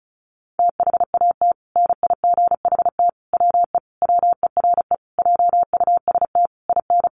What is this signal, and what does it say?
Morse code, heard as a keyed tone, about 34 wpm, 700 Hz. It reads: THAT NIGHT WE WERE JUST IN
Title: Sentences from Top 300 Words 35wpm